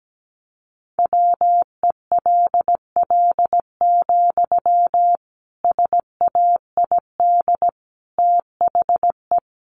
Morse code, heard as a keyed tone, about 17 wpm, 700 Hz. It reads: WELL, SAID THE
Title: Sentences from Top 400 Words 17wpm